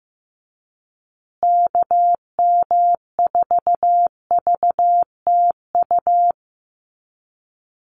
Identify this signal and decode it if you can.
Morse code, heard as a keyed tone, about 15 wpm, 700 Hz. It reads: KM4VTU
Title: USA Call Signs Repeated 2x 15wpm